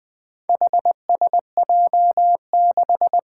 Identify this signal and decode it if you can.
Morse code, heard as a keyed tone, about 20 wpm, 700 Hz. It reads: HSJ6
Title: Four Challenging Letter-Number Combinations 20wpm